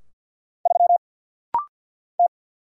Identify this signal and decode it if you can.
Morse code, heard as a keyed tone, about 50 wpm, 690 Hz. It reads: 3 T
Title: Single Letter-Number - Rapid-Fire 50wpm